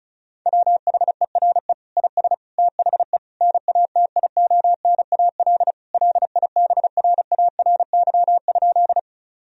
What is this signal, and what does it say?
Morse code, heard as a keyed tone, about 34 wpm, 700 Hz. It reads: WHERE IS THE NATIONAL LIBRARY?
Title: Sentences from Top 1000 Words - Review Part 4 - 35wpm